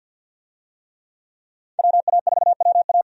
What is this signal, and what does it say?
Morse code, heard as a keyed tone, about 46 wpm, 700 Hz. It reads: WA4WA
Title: USA Call Signs - 5 Characters - 50wpm